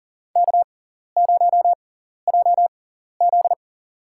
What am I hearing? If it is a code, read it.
Morse code, heard as a keyed tone, about 40 wpm, spaced as for 14 wpm, 700 Hz. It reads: K0JZ
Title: USA Call Signs - 4 Characters - 6x Character Spacing - 40wpm